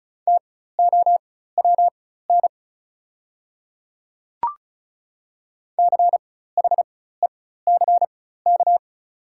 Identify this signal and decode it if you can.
Morse code, heard as a keyed tone, about 35 wpm, spaced as for 16 wpm, 690 Hz. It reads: TOWN CHECK
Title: 4x Character Spacing - Top 500 Words - 35wpm